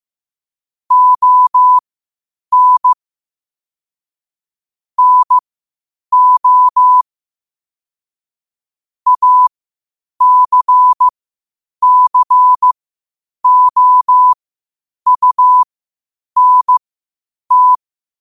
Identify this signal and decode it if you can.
Morse code, heard as a keyed tone, about 15 wpm, spaced as for 8 wpm, 1000 Hz. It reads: ON NO ACCOUNT
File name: qotdfiles / 8wpm